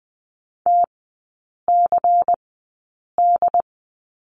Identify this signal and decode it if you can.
Morse code, heard as a keyed tone, about 20 wpm, spaced as for 8 wpm, 690 Hz. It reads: TCD
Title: 093 Lesson 16e - Letter c - Three Characters 20wpm